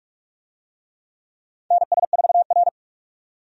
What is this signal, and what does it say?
Morse code, heard as a keyed tone, about 42 wpm, 700 Hz. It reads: NS4R
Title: USA Call Signs - 4 Characters - 45wpm